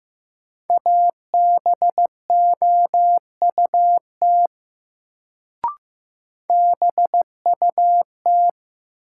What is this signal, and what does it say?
Morse code, heard as a keyed tone, about 15 wpm, 700 Hz. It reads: ABOUT BUT
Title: Sentences from Top 100 Words - New Word Drill - 15wpm